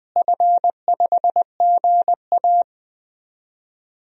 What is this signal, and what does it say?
Morse code, heard as a keyed tone, about 20 wpm, 700 Hz. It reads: F5GA
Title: DXCC-100 Call Signs - 4 Characters - Repeated 3x - 20wpm